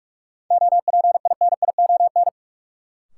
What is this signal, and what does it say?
Morse code, heard as a keyed tone, about 42 wpm, 700 Hz. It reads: OPINION
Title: Sentences from Top 1200 Words - New Word Drill Alt - 45wpm